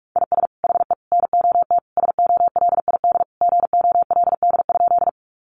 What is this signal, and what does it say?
Morse code, heard as a keyed tone, about 44 wpm, 700 Hz. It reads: IS HE NOT SOLID GOLD?
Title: Sentences from Top 1500 Words 45wpm